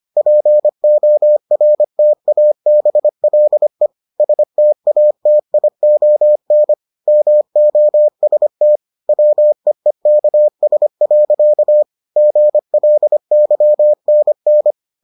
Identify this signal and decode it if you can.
Morse code, heard as a keyed tone, about 25 wpm, 600 Hz. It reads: PORTABLE STATION MOST WEEKS. GLYNN